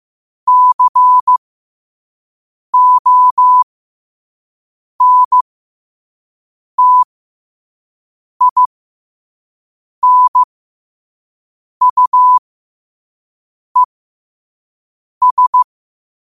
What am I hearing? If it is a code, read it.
Morse code, heard as a keyed tone, about 15 wpm, spaced as for 5 wpm, 1000 Hz. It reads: CONTINUES